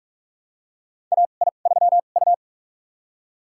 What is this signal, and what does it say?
Morse code, heard as a keyed tone, about 46 wpm, spaced as for 33 wpm, 700 Hz. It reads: AI3U
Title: USA Call Signs - 4 Characters - 2x Character Spacing - 45wpm